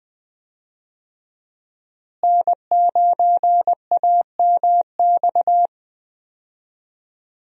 Morse code, heard as a keyed tone, about 20 wpm, 700 Hz. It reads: N9AMX
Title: USA Call Signs - 5 Characters - Repeated 3x - 20wpm